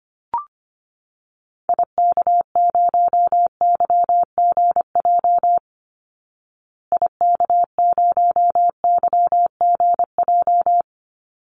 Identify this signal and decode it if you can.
Morse code, heard as a keyed tone, about 25 wpm, 700 Hz. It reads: IK0YGJ IK0YGJ
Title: Call Signs Repeated 2x - 25wpm